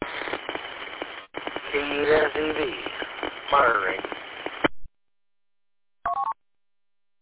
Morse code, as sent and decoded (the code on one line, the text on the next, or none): none